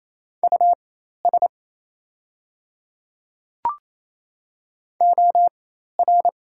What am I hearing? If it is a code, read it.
Morse code, heard as a keyed tone, about 28 wpm, spaced as for 13 wpm, 700 Hz. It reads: US OR
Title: 4x Character Spacing - Top 200 Words - 28wpm